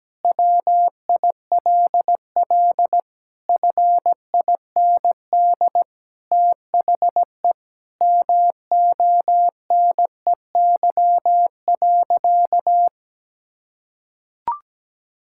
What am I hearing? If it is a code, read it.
Morse code, heard as a keyed tone, about 17 wpm, 700 Hz. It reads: WILL FIND THE MONEY.
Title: Sentences from Top 1600 Words 17wpm